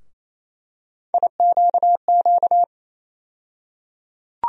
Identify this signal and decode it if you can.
Morse code, heard as a keyed tone, about 28 wpm, 700 Hz. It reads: IQQ
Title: USA Call Sign Postfixes - 3 Letters 28wpm